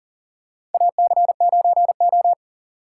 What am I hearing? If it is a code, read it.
Morse code, heard as a keyed tone, about 40 wpm, 700 Hz. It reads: AC9O
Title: USA Call Signs - 4 Characters - 3x Character Spacing - 40wpm